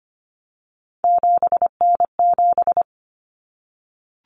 Morse code, heard as a keyed tone, about 25 wpm, 700 Hz. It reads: 7N7
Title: Common Cut Numbers - 3-Digit 25wpm